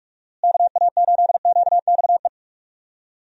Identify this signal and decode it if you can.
Morse code, heard as a keyed tone, about 46 wpm, 700 Hz. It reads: KA8QXE
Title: USA Call Signs - 6 Characters - Repeated 3x - 45wpm